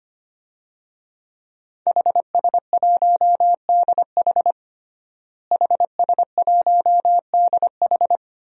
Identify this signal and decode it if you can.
Morse code, heard as a keyed tone, about 25 wpm, 700 Hz. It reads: HS1DH HS1DH
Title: DXCC-100 Call Signs - 5 Characters - Repeated 2x - 25wpm